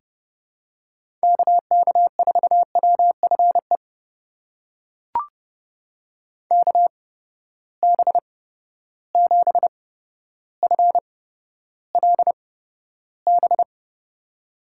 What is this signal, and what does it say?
Morse code, heard as a keyed tone, about 30 wpm, spaced as for 13 wpm, 700 Hz. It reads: KK4WFE KB7FLB
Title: USA Call Signs - 6 Characters - 8x Character Spacing - 30wpm